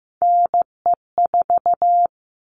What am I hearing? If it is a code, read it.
Morse code, heard as a keyed tone, about 15 wpm, 700 Hz. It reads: NE4
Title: All Cut Numbers - 3-Digit 15wpm